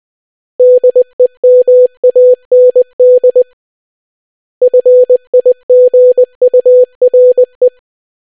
Morse code, heard as a keyed tone, about 20 wpm, 500 Hz. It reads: DEMAND FIGURE